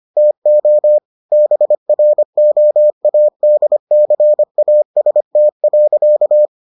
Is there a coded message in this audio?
Yes: Morse code from 0.2 to 6.5 s, heard as a keyed tone, about 25 wpm, 600 Hz. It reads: TO BROADCAST.